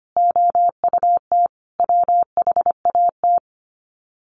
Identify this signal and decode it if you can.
Morse code, heard as a keyed tone, about 25 wpm, 700 Hz. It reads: OUT WHAT
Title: Sets of 5 Words 25wpm